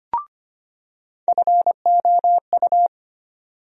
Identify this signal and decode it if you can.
Morse code, heard as a keyed tone, about 25 wpm, 700 Hz. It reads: FOU
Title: USA Call Sign Postfixes - 3 Letters 25wpm